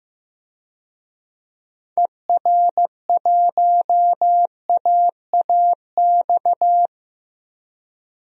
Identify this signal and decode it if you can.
Morse code, heard as a keyed tone, about 16 wpm, 700 Hz. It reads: ER1AAX
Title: DXCC-100 Call Signs - 6 Characters - Repeated 2x - 15wpm